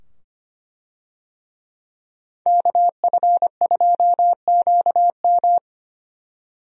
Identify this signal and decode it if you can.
Morse code, heard as a keyed tone, about 25 wpm, 690 Hz. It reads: KF2QM
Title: USA Call Signs - 5 Characters - 25wpm